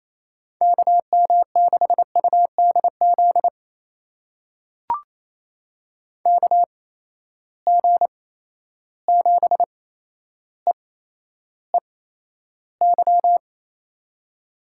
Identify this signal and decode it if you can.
Morse code, heard as a keyed tone, about 28 wpm, spaced as for 12 wpm, 700 Hz. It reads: KM6UDZ KG7EEY
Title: USA Call Signs - 6 Characters - 8x Character Spacing - 28wpm